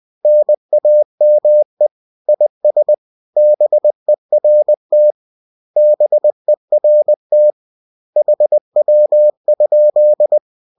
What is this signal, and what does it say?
Morse code, heard as a keyed tone, about 20 wpm, 600 Hz. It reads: NAME IS BERT BERT HW?